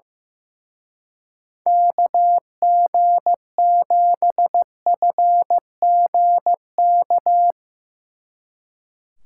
Morse code, heard as a keyed tone, about 16 wpm, 700 Hz. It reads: KG7FGK
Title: Speed Racing - USA 2x3 Call Signs 15wpm